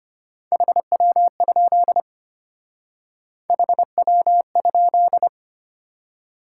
Morse code, heard as a keyed tone, about 30 wpm, 700 Hz. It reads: HW? HW?
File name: practice / 25wpm / Speed-Racing Top 20 QSO Elements - 25wpm